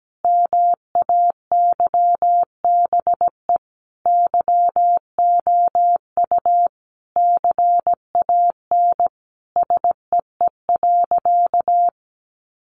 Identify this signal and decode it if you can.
Morse code, heard as a keyed tone, about 17 wpm, 700 Hz. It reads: MAYBE YOU CAN SEE.
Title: Sentences from Top 1000 Words 17wpm